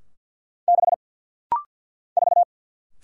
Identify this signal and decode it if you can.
Morse code, heard as a keyed tone, about 50 wpm, 700 Hz. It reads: 6 4
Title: Single Letter-Number - Rapid-Fire 50wpm